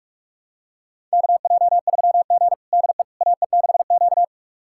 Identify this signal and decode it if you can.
Morse code, heard as a keyed tone, about 46 wpm, 700 Hz. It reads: KJ3G DE AE6Q
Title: USA Callsign DE Callsign - 4 Characters - 45wpm